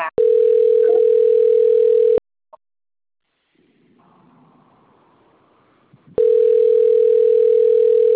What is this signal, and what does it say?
Ringback tone (North America), 440 + 480 Hz.
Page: Bebedouro 3